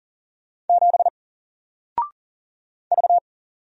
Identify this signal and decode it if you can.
Morse code, heard as a keyed tone, about 40 wpm, 700 Hz. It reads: 7 V
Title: Single Letter-Number Alt 40wpm